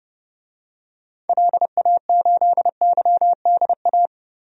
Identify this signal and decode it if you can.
Morse code, heard as a keyed tone, about 30 wpm, 700 Hz. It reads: LA8YDA
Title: DXCC-100 Call Signs - 6 Characters - 30wpm